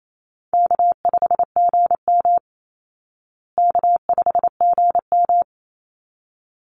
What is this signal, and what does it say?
Morse code, heard as a keyed tone, about 28 wpm, 700 Hz. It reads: K5GM K5GM
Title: Easy Call Signs Repeated 2x - 28wpm